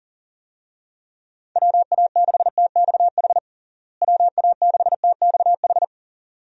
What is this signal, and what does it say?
Morse code, heard as a keyed tone, about 40 wpm, 700 Hz. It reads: WA6TXH WA6TXH
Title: USA Call Signs - 6 Characters - Repeated 2x - 40wpm